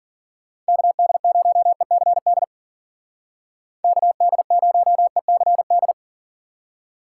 Morse code, heard as a keyed tone, about 53 wpm, 700 Hz. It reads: KD0ECD KD0ECD